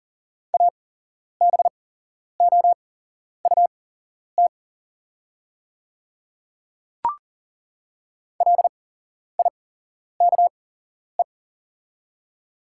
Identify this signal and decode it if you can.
Morse code, heard as a keyed tone, about 38 wpm, spaced as for 11 wpm, 700 Hz. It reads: ABOUT LIKE
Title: 8x Character Spacing - Top 100 Words - 40wpm